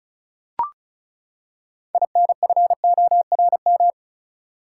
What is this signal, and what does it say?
Morse code, heard as a keyed tone, about 36 wpm, 700 Hz. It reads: INFORM
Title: Six Letter Words - 35wpm